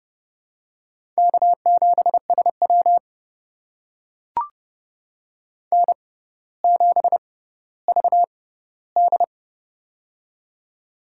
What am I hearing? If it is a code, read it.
Morse code, heard as a keyed tone, about 32 wpm, spaced as for 16 wpm, 700 Hz. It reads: K7SW N7VD
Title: USA Call Signs - 4 Characters - 6x Character Spacing - 30wpm